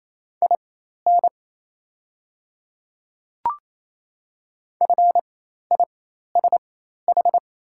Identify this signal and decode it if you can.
Morse code, heard as a keyed tone, about 28 wpm, spaced as for 13 wpm, 700 Hz. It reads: IN FISH
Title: 4x Character Spacing - Top 300 Words - 28wpm